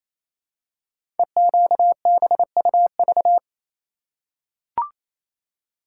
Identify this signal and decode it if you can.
Morse code, heard as a keyed tone, about 28 wpm, 690 Hz. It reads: EQBUV
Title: Code Groups - 5 Letters - 28wpm